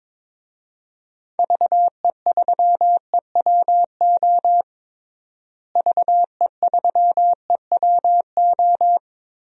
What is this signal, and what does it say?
Morse code, heard as a keyed tone, about 22 wpm, 700 Hz. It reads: VE3EWO VE3EWO
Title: DXCC-100 Call Signs - 6 Characters - Repeated 3x - 22wpm